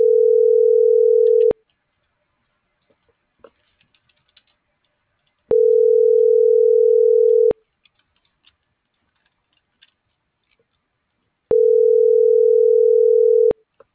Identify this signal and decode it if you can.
Ringback tone (North America), 440 + 480 Hz.